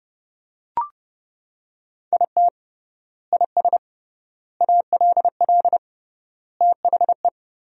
Morse code, heard as a keyed tone, about 30 wpm, 700 Hz. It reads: IT IS ALL THE
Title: 3x Word-Spacing - Sentences from Top 200 Words - 30wpm